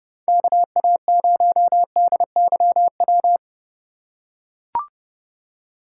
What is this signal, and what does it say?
Morse code, heard as a keyed tone, about 30 wpm, 700 Hz. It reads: KA0DYW